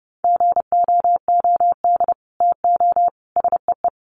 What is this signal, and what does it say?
Morse code, heard as a keyed tone, about 29 wpm, 700 Hz. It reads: GOOD TO SEE
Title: Sets of 4 Words 30wpm